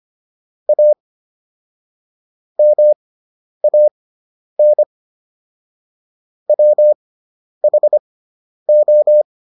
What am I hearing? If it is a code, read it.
Morse code, heard as a keyed tone, about 25 wpm, spaced as for 10 wpm, 600 Hz. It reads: A MAN WHO